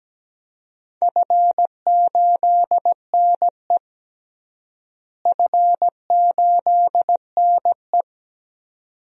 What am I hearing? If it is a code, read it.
Morse code, heard as a keyed tone, about 17 wpm, 700 Hz. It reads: F8NE F8NE
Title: DXCC-100 Call Signs - 4 Characters - Repeated 2x - 17wpm